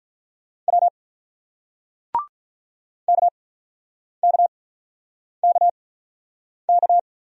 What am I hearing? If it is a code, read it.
Morse code, heard as a keyed tone, about 48 wpm, spaced as for 9 wpm, 700 Hz. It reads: W KKKK